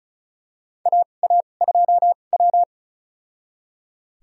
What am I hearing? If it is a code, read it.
Morse code, heard as a keyed tone, about 36 wpm, spaced as for 25 wpm, 700 Hz. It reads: AA2W
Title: USA Call Signs - 2x Character Spacing - 35wpm